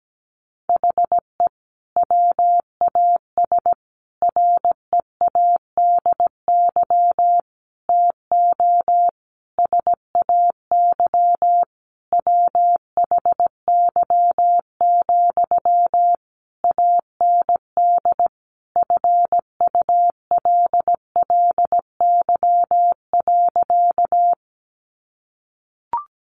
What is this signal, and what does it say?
Morse code, heard as a keyed tone, about 17 wpm, 700 Hz. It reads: HE WAS READY TO SAY WHY, AND FULLY.
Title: Sentences from Top 1300 Words 17wpm